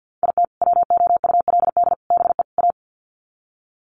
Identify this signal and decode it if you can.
Morse code, heard as a keyed tone, about 48 wpm, 700 Hz. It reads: IT WOULD BE A
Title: Sets of 4 Words 50wpm